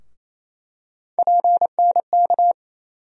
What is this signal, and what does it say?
Morse code, heard as a keyed tone, about 28 wpm, 700 Hz. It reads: PNK
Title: USA Call Sign Postfixes - 3 Letters 28wpm